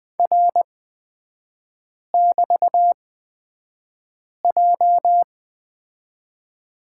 Morse code, heard as a keyed tone, about 20 wpm, spaced as for 5 wpm, 700 Hz. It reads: R<BT>J
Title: 229 Lesson 37e - Letter x - Three Characters 20wpm